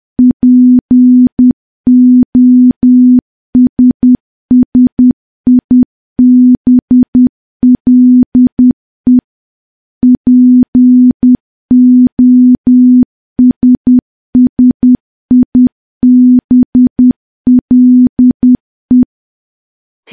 Morse code, heard as a keyed tone, about 10 wpm, 250 Hz. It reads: POSSIBLE POSSIBLE
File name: morse / morse-training-10WPM